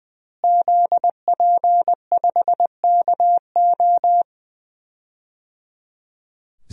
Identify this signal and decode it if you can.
Morse code, heard as a keyed tone, about 21 wpm, 700 Hz. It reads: ZP5KO
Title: DXCC-100 Call Signs - 5 Characters - Repeated 3x - 20wpm